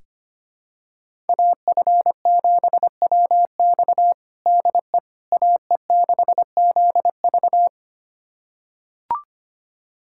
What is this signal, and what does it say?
Morse code, heard as a keyed tone, about 25 wpm, 700 Hz. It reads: AF7WX DE AE6ZV
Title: USA Callsign DE Callsign - 5 Characters - 25wpm